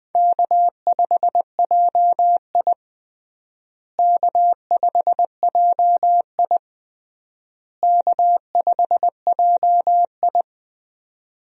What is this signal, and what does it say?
Morse code, heard as a keyed tone, about 20 wpm, 700 Hz. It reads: K5JI K5JI K5JI
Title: USA Call Signs - 4 Characters - Repeated 3x - 20wpm